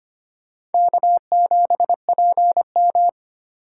Morse code, heard as a keyed tone, about 32 wpm, 700 Hz. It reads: K7PM